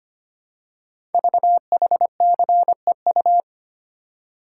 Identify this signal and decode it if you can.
Morse code, heard as a keyed tone, about 25 wpm, 700 Hz. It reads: VHCEU
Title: Code Groups - 5 Letters - 25wpm